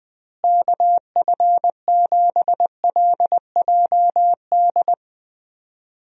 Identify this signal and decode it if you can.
Morse code, heard as a keyed tone, about 20 wpm, 700 Hz. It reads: KF7LJD
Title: USA Call Signs - 6 Characters - Repeated 2x - 20wpm